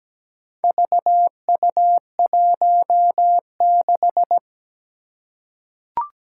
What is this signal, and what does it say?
Morse code, heard as a keyed tone, about 17 wpm, 700 Hz. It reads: VU16
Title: Four Challenging Letter-Number Combinations 17wpm